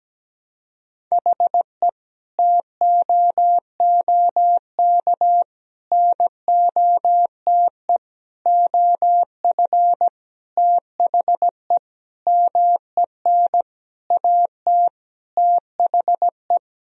Morse code, heard as a keyed tone, about 17 wpm, 700 Hz. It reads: HE TOOK NOTE OF THE MEN AT THE
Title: Sentences from Top 1000 Words - Review Part 2 - 17wpm